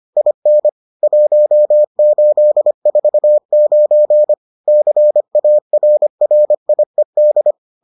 Morse code, heard as a keyed tone, about 25 wpm, 600 Hz. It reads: IN 1849 CARRIED